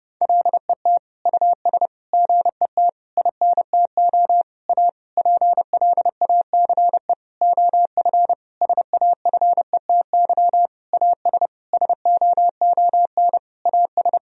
Morse code, heard as a keyed tone, about 30 wpm, 700 Hz. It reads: LET US GET INTO A PLACE OF SAFETY AS SOON AS
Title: Sentences from Top 1000 Words - Review Part 4 - 30wpm